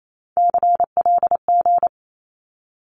Morse code, heard as a keyed tone, about 28 wpm, 700 Hz. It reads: CLG
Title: Abbreviations 28wpm